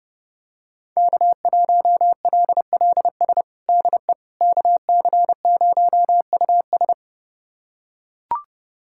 Morse code, heard as a keyed tone, about 32 wpm, 700 Hz. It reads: K1LLS DE KC0US
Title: USA Callsign DE Callsign - 5 Characters - 30wpm